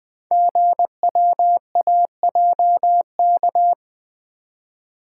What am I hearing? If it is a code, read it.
Morse code, heard as a keyed tone, about 20 wpm, 700 Hz. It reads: GWAJK
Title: Code Groups - 5 Letter:Numbers - 20wpm